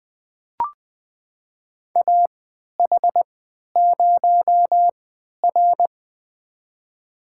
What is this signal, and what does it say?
Morse code, heard as a keyed tone, about 20 wpm, spaced as for 11 wpm, 700 Hz. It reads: AH0R
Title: USA Call Signs - 4 Characters - 3x Character Spacing - 20wpm